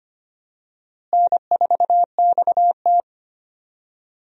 Morse code, heard as a keyed tone, about 25 wpm, 700 Hz. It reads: N4XT
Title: USA Call Signs - 4 Characters - Repeated 2x - 25wpm